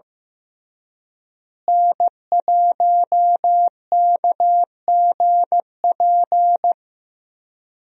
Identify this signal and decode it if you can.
Morse code, heard as a keyed tone, about 15 wpm, 700 Hz. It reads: N1KGP
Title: USA Call Signs - 5 Characters - Repeated 3x - 15wpm